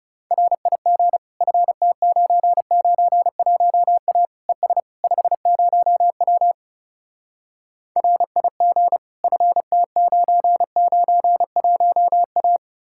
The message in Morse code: RIG FT991A ES 50W RIG FT991A

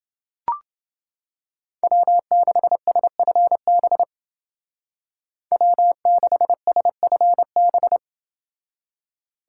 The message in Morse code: W6SFB W6SFB

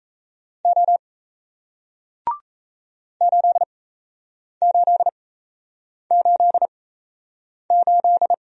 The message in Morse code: O 8888

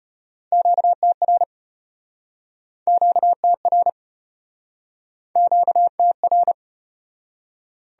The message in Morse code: QTR QTR QTR

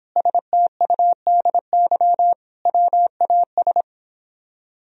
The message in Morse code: STUDY WAS

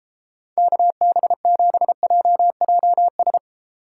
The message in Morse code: KB7JJS

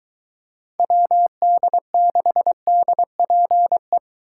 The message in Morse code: WD6DPE

